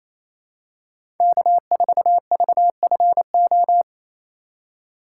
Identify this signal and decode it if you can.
Morse code, heard as a keyed tone, about 27 wpm, 700 Hz. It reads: K4VFO
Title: USA Call Signs - 4x Character Spacing - 28wpm